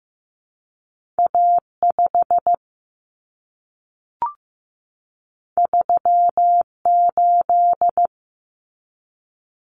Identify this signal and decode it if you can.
Morse code, heard as a keyed tone, about 15 wpm, 700 Hz. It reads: A5 38
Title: Common Cut Numbers - 2-Digit 15wpm